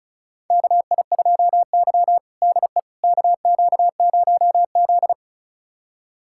Morse code, heard as a keyed tone, about 34 wpm, 700 Hz. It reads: KI2Y DE KQ0Z
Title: USA Callsign DE Callsign - 4 Characters - 35wpm